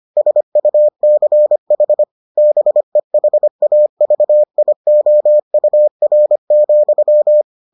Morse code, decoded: SUCH BEHAVIOUR,

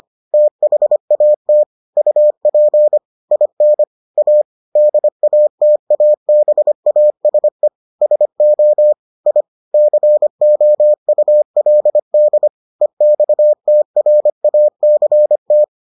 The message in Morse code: THAT UP IN A DATABASE SO I COULD EXTRACT

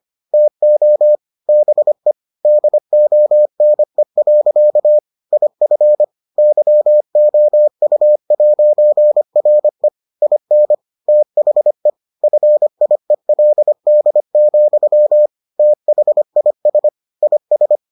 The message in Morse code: TO BE DONE. IF YOU'RE IN THE FIELD, THIS IS

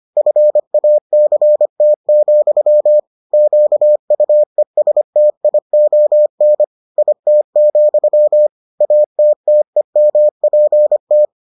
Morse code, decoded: FACT, QUESTION IT, ATTEMPT

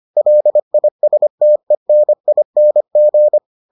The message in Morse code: LISTENING

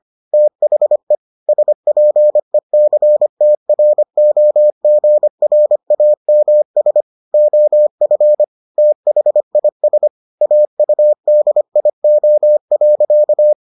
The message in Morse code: THE SPECTROGRAMS OF THIS AUDIO.

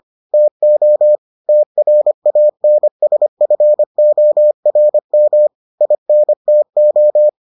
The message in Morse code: TO TRANSFORM INTO

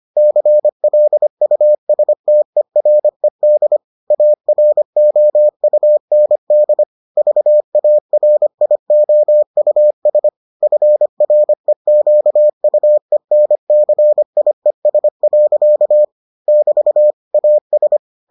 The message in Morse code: CLUSTERED AROUND VARIOUS FREQUENCIES. <BT> AS

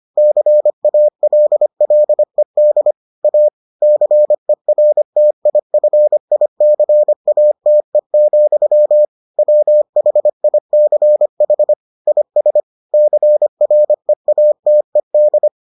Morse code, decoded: CALLED A CERTIFICATE, WHICH IS CREATED

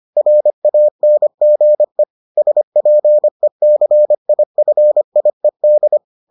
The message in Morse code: RANGE SPECIFIED

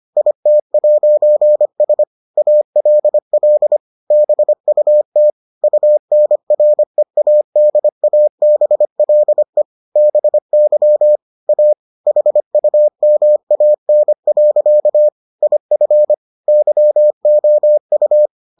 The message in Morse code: IT'S ALL BUT UNREADABLE BY A HUMAN. IF YOU

